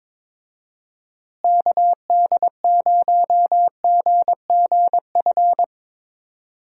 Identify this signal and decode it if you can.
Morse code, heard as a keyed tone, about 22 wpm, 700 Hz. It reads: KD0GGF
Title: USA Call Signs - 6 Characters - Repeated 2x - 22wpm